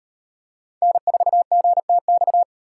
Morse code, heard as a keyed tone, about 36 wpm, 700 Hz. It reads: N4GTX